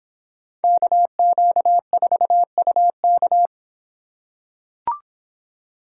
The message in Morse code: KQ4UK

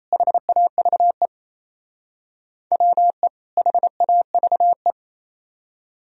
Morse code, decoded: HAVE WE HAVE

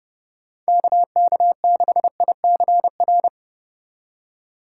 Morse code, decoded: KK6ICR

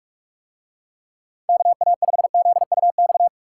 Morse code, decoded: KA5ZUX

